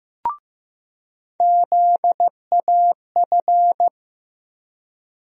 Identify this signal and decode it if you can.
Morse code, heard as a keyed tone, about 15 wpm, 700 Hz. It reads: ZAF
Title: USA Call Sign Postfixes - 3 Letters 15wpm